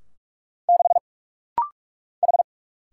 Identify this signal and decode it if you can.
Morse code, heard as a keyed tone, about 46 wpm, 700 Hz. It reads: 6 H
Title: Single Letter-Number Alt - Rapid-Fire 45wpm